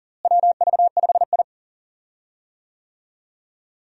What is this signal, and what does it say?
Morse code, heard as a keyed tone, about 40 wpm, 700 Hz. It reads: WV5I
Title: USA Call Signs - 4 Characters - Repeated 2x - 40wpm